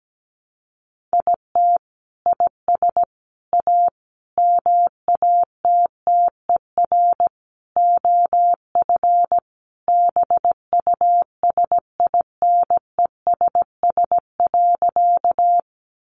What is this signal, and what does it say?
Morse code, heard as a keyed tone, about 17 wpm, 700 Hz. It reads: IT IS A MATTER OF BUSINESS.
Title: Sentences from Top 700 Words 17wpm